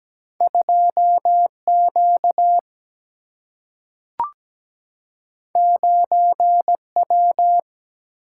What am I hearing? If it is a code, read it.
Morse code, heard as a keyed tone, about 17 wpm, 700 Hz. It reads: 2 Q 9 W
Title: Single-Letter Single-Number 17wpm